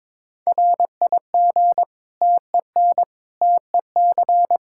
Morse code, heard as a keyed tone, about 22 wpm, 700 Hz. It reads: RIG TEN TEC
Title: Practice QSO - Exchange 2 - Info - 22wpm